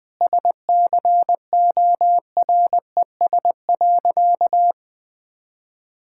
Morse code, heard as a keyed tone, about 20 wpm, 700 Hz. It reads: SCORES.
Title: All-Age Appropriate Jokes - 20wpm